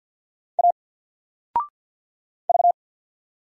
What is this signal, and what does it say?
Morse code, heard as a keyed tone, about 50 wpm, 700 Hz. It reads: A V
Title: Single Letter-Number 50wpm